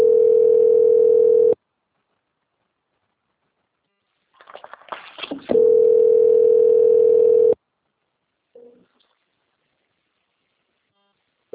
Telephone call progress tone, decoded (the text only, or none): ringback tone